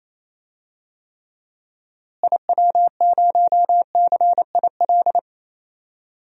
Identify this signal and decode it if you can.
Morse code, heard as a keyed tone, about 28 wpm, 700 Hz. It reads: IW0CIL
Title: DXCC-100 Call Signs - 6 Characters - Repeated 2x - 28wpm